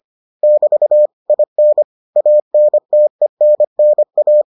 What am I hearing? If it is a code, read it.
Morse code, heard as a keyed tone, about 25 wpm, 600 Hz. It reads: <BT> IN ANTENNA